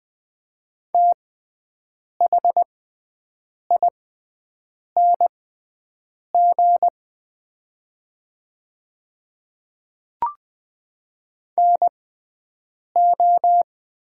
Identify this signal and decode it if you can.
Morse code, heard as a keyed tone, about 20 wpm, spaced as for 7 wpm, 700 Hz. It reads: THING NO
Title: 6x Character Spacing - Top 100 Words - 20wpm